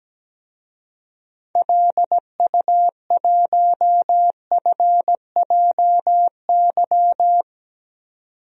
Morse code, heard as a keyed tone, about 17 wpm, 700 Hz. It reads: LU1FJY
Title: DXCC-100 Call Signs - 6 Characters - 17wpm